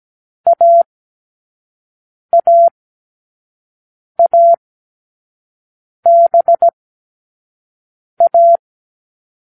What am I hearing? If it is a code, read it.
Morse code, heard as a keyed tone, about 17 wpm, spaced as for 5 wpm, 680 Hz. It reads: AAABA